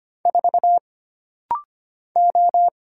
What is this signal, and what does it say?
Morse code, heard as a keyed tone, about 25 wpm, 700 Hz. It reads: 4 O
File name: Single Character (Letter:Number:Punctuation:Prosign) - Rapid-Fire - 25wpm